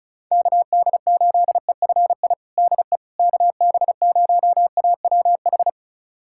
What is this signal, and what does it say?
Morse code, heard as a keyed tone, about 36 wpm, 700 Hz. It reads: KD8EFI DE KB0AWH
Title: USA Callsign DE Callsign - 6 Characters - 35wpm